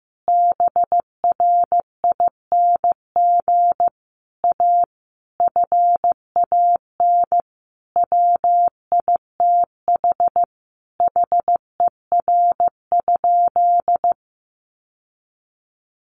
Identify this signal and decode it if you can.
Morse code, heard as a keyed tone, about 15 wpm, 700 Hz. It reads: BRING A FAN WITH HER?
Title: Sentences from Top 1300 Words 15wpm